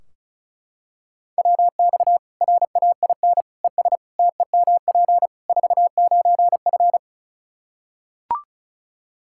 Morse code, heard as a keyed tone, about 36 wpm, 700 Hz. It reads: WX RAIN ES TEMP 49F
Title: Practice QSO - Exchange 2 - Info - 35wpm